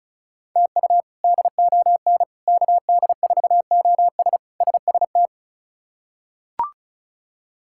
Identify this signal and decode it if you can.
Morse code, heard as a keyed tone, about 36 wpm, 700 Hz. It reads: TU DON KD4OS SST
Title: K1USN SST QSO Practice - 35wpm